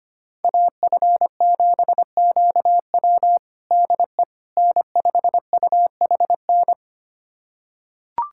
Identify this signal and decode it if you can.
Morse code, heard as a keyed tone, about 25 wpm, 700 Hz. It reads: AF7QW DE N5UHN
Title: USA Callsign DE Callsign - 5 Characters - 25wpm